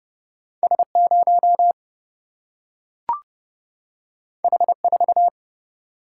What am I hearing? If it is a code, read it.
Morse code, heard as a keyed tone, about 30 wpm, 700 Hz. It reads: S 0 H 4
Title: Single-Letter Single-Number 30wpm